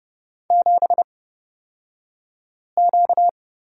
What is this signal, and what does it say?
Morse code, heard as a keyed tone, about 30 wpm, spaced as for 5 wpm, 700 Hz. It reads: 7Q
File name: course / 30:10wpm / 235 Lesson 38d - Letter q - Two Characters 30wpm